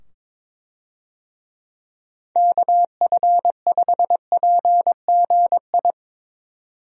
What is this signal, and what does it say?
Morse code, heard as a keyed tone, about 23 wpm, 700 Hz. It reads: KF5PGI
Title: USA Call Signs - 6 Characters - 22wpm